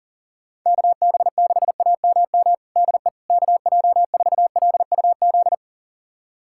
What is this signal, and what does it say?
Morse code, heard as a keyed tone, about 40 wpm, 700 Hz. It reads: KB6AMM DE KJ4LUZ
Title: USA Callsign DE Callsign - 6 Characters - 40wpm